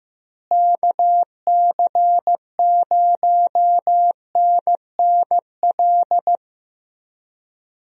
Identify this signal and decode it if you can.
Morse code, heard as a keyed tone, about 16 wpm, 700 Hz. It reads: KC0NNL
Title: USA Call Signs - 6 Characters - Repeated 3x - 15wpm